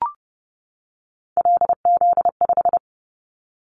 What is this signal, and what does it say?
Morse code, heard as a keyed tone, about 30 wpm, 700 Hz. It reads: LZ5
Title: Callsign Prefixes for Top 100 DXCC - 30wpm